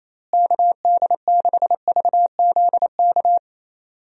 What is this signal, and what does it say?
Morse code, heard as a keyed tone, about 28 wpm, 700 Hz. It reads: KD6VZK
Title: USA Call Signs - 6 Characters - Repeated 3x - 28wpm